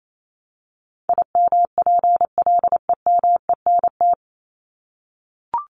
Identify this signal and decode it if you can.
Morse code, heard as a keyed tone, about 28 wpm, 700 Hz. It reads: IMPLEMENT
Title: Top 1501-1750 Words 28wpm